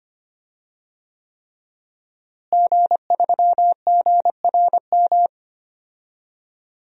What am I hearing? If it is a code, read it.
Morse code, heard as a keyed tone, about 25 wpm, 700 Hz. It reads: G3GRM
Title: DXCC-100 Call Signs - 5 Characters - Repeated 3x - 25wpm